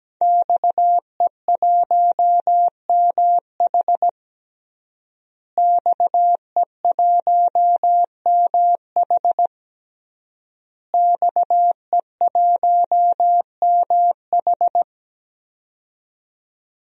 Morse code, heard as a keyed tone, about 17 wpm, 700 Hz. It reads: XE1MH XE1MH XE1MH
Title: DXCC-100 Call Signs - 5 Characters - Repeated 3x - 17wpm